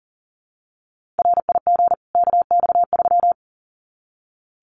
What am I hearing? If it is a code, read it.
Morse code, heard as a keyed tone, about 40 wpm, 700 Hz. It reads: RIG KX3
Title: Common QSO Phrases 40wpm